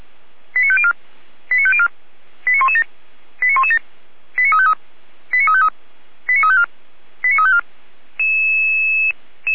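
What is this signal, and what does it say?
Five-tone selective call (ZVEI), heard as quick runs of five tones, tones of about 70 ms. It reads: 89504 89504 89117 89117 89353 89353 89354 89354
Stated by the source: ZVEI five-tone call 89164 received by SDR, from the audio server of the Feuerwehr Orlamünde fire brigade.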